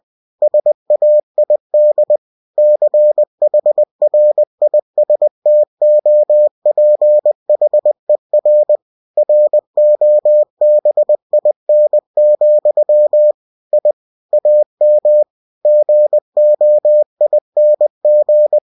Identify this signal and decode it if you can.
Morse code, heard as a keyed tone, about 20 wpm, 600 Hz. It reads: SAID CHRISTOPHER ROBIN, I AM GOING
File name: cw-books / winnie-the-pooh / winnie-20wpm-page-0102